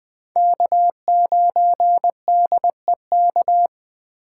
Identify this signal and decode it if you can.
Morse code, heard as a keyed tone, about 20 wpm, 700 Hz. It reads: K9DEK
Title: Speed Racing - USA 1x3 Call Signs 17wpm